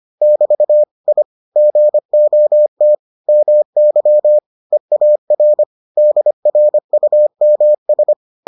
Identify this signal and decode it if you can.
Morse code, heard as a keyed tone, about 25 wpm, 600 Hz. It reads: <BT> I GOT MY EAR DRUMS